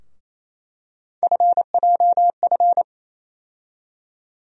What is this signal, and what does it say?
Morse code, heard as a keyed tone, about 27 wpm, 700 Hz. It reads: FJF
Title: USA Call Sign Postfixes - 3 Letters 28wpm